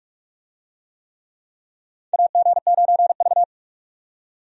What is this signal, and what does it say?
Morse code, heard as a keyed tone, about 43 wpm, 700 Hz. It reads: AG9V
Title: USA Call Signs - 4 Characters - 45wpm